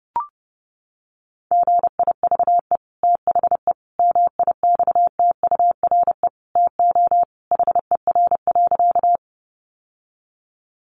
Morse code, heard as a keyed tone, about 30 wpm, 700 Hz. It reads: GIVE THE MIXTURE TO HER.
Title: Sentences from Top 1500 Words 30wpm